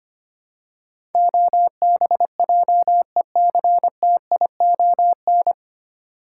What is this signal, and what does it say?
Morse code, heard as a keyed tone, about 25 wpm, 700 Hz. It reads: OBJECTION
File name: Nine Letter Words - 25wpm